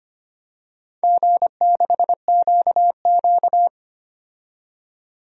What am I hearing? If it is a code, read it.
Morse code, heard as a keyed tone, about 25 wpm, 700 Hz. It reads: G6QQ
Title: DXCC-100 Call Signs - 4 Characters - 25wpm